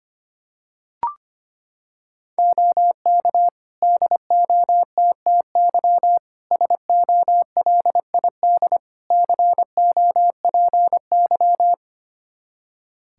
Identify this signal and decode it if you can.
Morse code, heard as a keyed tone, about 25 wpm, 700 Hz. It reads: OK DOTTY SOLID COPY
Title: K1USN SST QSO Practice 25wpm